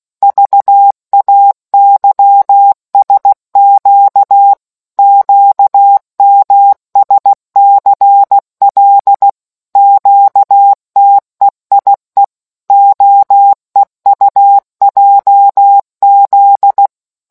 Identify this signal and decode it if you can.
Morse code, heard as a keyed tone, about 16 wpm, 800 Hz. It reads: VAYSQ QMSCL QTEIE OEUJZ